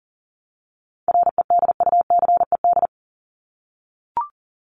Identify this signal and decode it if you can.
Morse code, heard as a keyed tone, about 40 wpm, 700 Hz. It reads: REDUCED
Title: ED Suffix - Top 100 Words with Base Words 40wpm